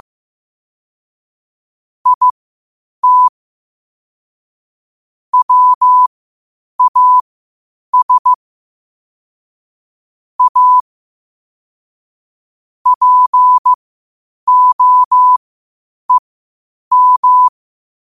Morse code, heard as a keyed tone, about 15 wpm, spaced as for 8 wpm, 1000 Hz. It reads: IT WAS A POEM
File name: qotdfiles / 8wpm